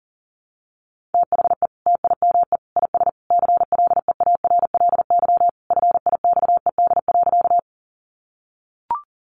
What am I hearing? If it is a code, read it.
Morse code, heard as a keyed tone, about 40 wpm, 700 Hz. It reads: THE TIME IS CLEARLY FIXED.
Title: Sentences from Top 1300 Words 40wpm